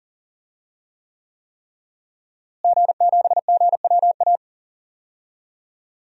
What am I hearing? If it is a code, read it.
Morse code, heard as a keyed tone, about 39 wpm, 700 Hz. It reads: G7GWA
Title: DXCC-100 Call Signs - 5 Characters - Repeated 3x - 40wpm